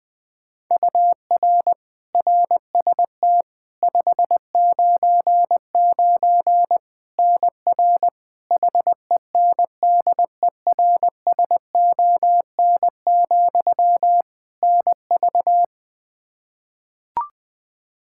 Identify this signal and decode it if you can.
Morse code, heard as a keyed tone, about 20 wpm, 700 Hz. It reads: UR RST 599 NR HENDERSON, NV
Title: Practice QSO - Exchange 1 - Beginning - 20wpm